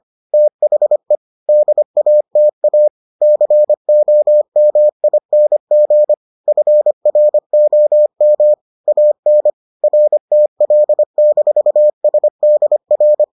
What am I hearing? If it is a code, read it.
Morse code, heard as a keyed tone, about 25 wpm, 600 Hz. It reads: THE DATA COMING FROM AN RTL-SDR